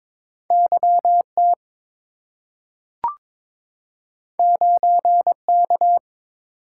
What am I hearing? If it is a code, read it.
Morse code, heard as a keyed tone, about 22 wpm, 700 Hz. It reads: YT 9K
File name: practice / 22wpm / Two Letter-Number Combinations - 22wpm